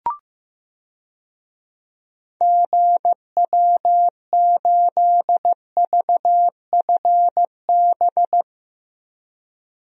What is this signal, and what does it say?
Morse code, heard as a keyed tone, about 15 wpm, 700 Hz. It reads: GW8VFB
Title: DXCC-100 Call Signs - 6 Characters - 15wpm